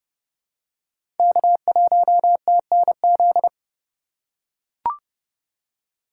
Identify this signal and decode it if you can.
Morse code, heard as a keyed tone, about 30 wpm, 700 Hz. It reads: K1TNZ
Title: USA Call Signs - 5 Characters - 2x Character Spacing - 30wpm